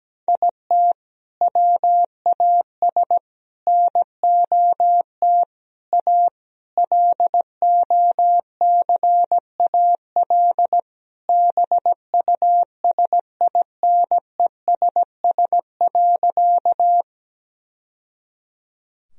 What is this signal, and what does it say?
Morse code, heard as a keyed tone, about 17 wpm, 700 Hz. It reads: IT WAS NOT A LOCAL BUSINESS.
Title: Sentences from Top 1000 Words - Review Part 3 - 17wpm